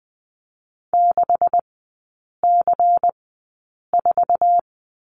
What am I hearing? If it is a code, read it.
Morse code, heard as a keyed tone, about 20 wpm, spaced as for 8 wpm, 700 Hz. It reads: 6C4
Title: 127 Lesson 21e - Number 6 - Three Characters 20wpm